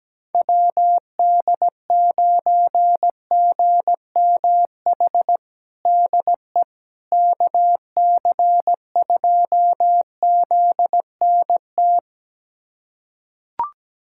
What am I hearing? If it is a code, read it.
Morse code, heard as a keyed tone, about 17 wpm, 700 Hz. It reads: WD9GMH DE KC2ZNT
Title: USA Callsign DE Callsign 17wpm